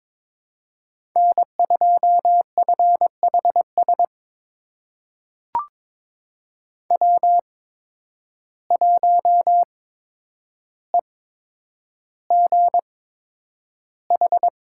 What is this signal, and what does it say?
Morse code, heard as a keyed tone, about 22 wpm, spaced as for 9 wpm, 700 Hz. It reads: N2FHS W1EGH
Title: USA Call Signs - 5 Characters - 8x Character Spacing - 22wpm